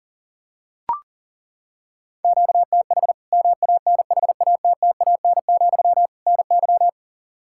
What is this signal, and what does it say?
Morse code, heard as a keyed tone, about 40 wpm, 700 Hz. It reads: QTH MANHATTAN, NY
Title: Practice QSO - Exchange 1 - Beginning - 40wpm